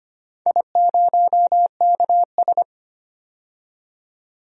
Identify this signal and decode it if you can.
Morse code, heard as a keyed tone, about 25 wpm, 700 Hz. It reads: I0KS
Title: DXCC-100 Call Signs - 4 Characters - 25wpm